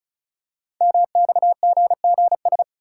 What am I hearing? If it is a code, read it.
Morse code, heard as a keyed tone, about 36 wpm, 690 Hz. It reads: MXGGS
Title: Code Groups - 5 Letters - 35wpm